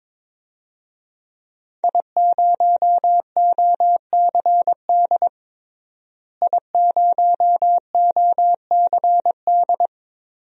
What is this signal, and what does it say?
Morse code, heard as a keyed tone, about 22 wpm, 700 Hz. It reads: I0OCD I0OCD
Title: DXCC-100 Call Signs - 5 Characters - Repeated 2x - 22wpm